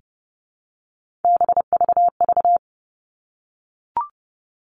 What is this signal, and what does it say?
Morse code, heard as a keyed tone, about 30 wpm, 700 Hz. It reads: BVV
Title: All Cut Numbers - 3-Digit 30wpm